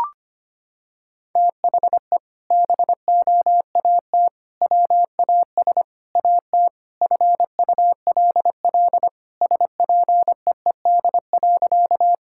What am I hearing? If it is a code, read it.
Morse code, heard as a keyed tone, about 25 wpm, 700 Hz. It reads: THE BOAT WAS AT FULL SPEED.
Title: Sentences from Top 1000 Words - Review Part 5 - 25wpm